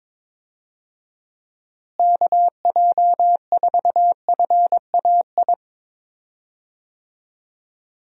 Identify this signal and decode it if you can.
Morse code, heard as a keyed tone, about 22 wpm, 700 Hz. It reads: KJ4FAI
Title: USA Call Signs - 6 Characters - Repeated 3x - 22wpm